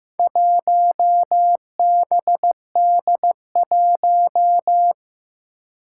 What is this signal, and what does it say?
Morse code, heard as a keyed tone, about 15 wpm, 700 Hz. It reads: 1BD1
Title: Four Challenging Letter-Number Combinations 15wpm